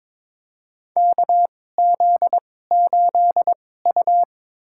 Morse code, heard as a keyed tone, about 22 wpm, spaced as for 16 wpm, 700 Hz. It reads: KZ8U
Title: USA Call Signs - 4 Characters - 2x Character Spacing - 22wpm